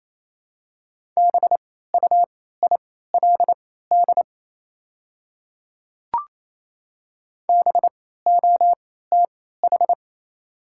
Morse code, heard as a keyed tone, about 28 wpm, spaced as for 16 wpm, 700 Hz. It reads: BUILD BOTH
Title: 3x Character Spacing - Top 300 Words - 28wpm